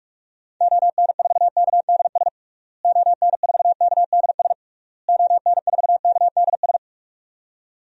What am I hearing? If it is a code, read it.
Morse code, heard as a keyed tone, about 46 wpm, 700 Hz. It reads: ON4KDS ON4KDS ON4KDS
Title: DXCC-100 Call Signs - 6 Characters - Repeated 3x - 45wpm